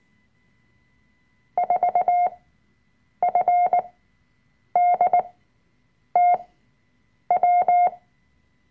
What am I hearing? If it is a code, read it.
Morse code, heard as a keyed tone, about 19 wpm, spaced as for 7 wpm, 720 Hz. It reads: 4FDTW
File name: Workshop Dateien / CW-Online-Kurs / Lektion 005